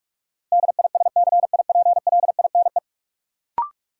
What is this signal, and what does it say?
Morse code, heard as a keyed tone, about 42 wpm, 700 Hz. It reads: DISCIPLINE
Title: Sentences from Top 900 Words - New Word Drill - 45wpm